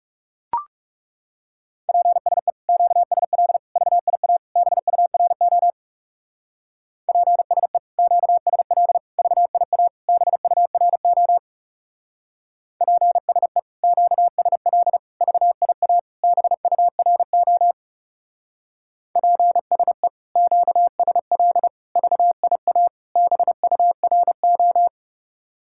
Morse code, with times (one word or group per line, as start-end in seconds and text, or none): 1.9-2.5 PSE
2.7-3.6 QSL
3.8-4.4 VIA
4.6-5.7 BURO
7.1-7.8 PSE
8.0-9.0 QSL
9.2-9.9 VIA
10.1-11.4 BURO
12.8-13.6 PSE
13.8-15.0 QSL
15.2-16.0 VIA
16.2-17.7 BURO
19.2-20.1 PSE
20.4-21.7 QSL
22.0-22.9 VIA
23.2-24.9 BURO